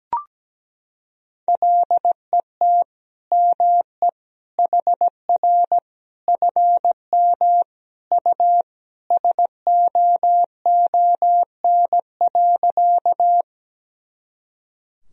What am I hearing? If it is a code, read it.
Morse code, heard as a keyed tone, about 17 wpm, 700 Hz. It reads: LET ME HR FM U SOON.
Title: Sentences from Top 400 Words and Abbreviations 17wpm